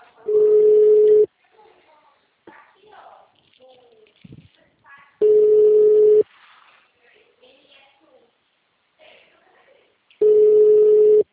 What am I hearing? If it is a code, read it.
Ringback tone (Europe, 425 Hz), with speech.